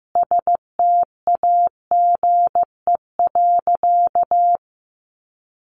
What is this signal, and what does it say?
Morse code, heard as a keyed tone, about 15 wpm, 700 Hz. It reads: STAGE.
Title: Sentences from Top 1200 Words 15wpm